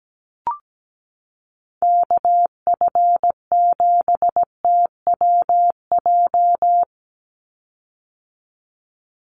Morse code, heard as a keyed tone, about 17 wpm, 700 Hz. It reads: KF7TWJ
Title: USA Call Signs 17wpm